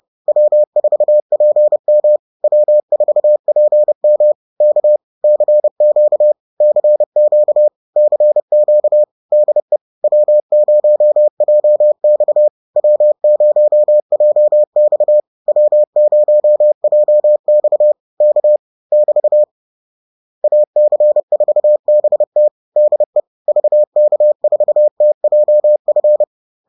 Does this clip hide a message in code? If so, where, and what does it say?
Yes: Morse code from 0.3 to 26.2 s, heard as a keyed tone, about 30 wpm, 600 Hz. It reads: W4PM W4PM K CQ CQ CQ DE W0JX W0JX W0JX K <BT> AC4BT DE VK4TJF